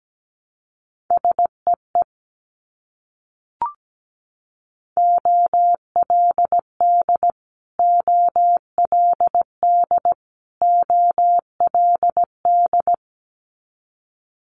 Morse code, heard as a keyed tone, about 17 wpm, 700 Hz. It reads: SEE OLD OLD OLD
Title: Multi-letter Phonemes Repeated 3x 17wpm